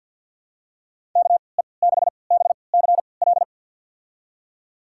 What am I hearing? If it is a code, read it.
Morse code, heard as a keyed tone, about 49 wpm, spaced as for 28 wpm, 700 Hz. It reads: KE6BCL
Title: USA Call Signs - 3x Character Spacing - 50wpm